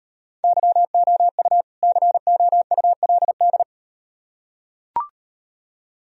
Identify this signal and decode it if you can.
Morse code, heard as a keyed tone, about 36 wpm, 700 Hz. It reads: YOU COULD